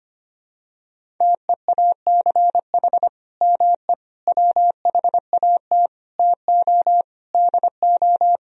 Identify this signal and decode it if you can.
Morse code, heard as a keyed tone, about 25 wpm, 700 Hz. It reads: TEACH ME WHAT TO DO
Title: Sentences from Top 500 Words and Abbreviations 25wpm